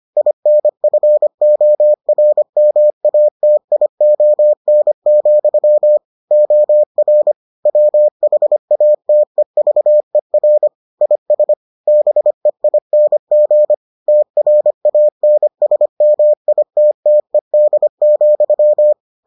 Morse code, heard as a keyed tone, about 25 wpm, 600 Hz. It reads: INFORMATION, OR WHATEVER IS BEING TRANSMITTED,